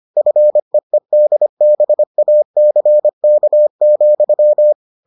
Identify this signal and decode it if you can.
Morse code, heard as a keyed tone, about 25 wpm, 600 Hz. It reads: FEEDBACK,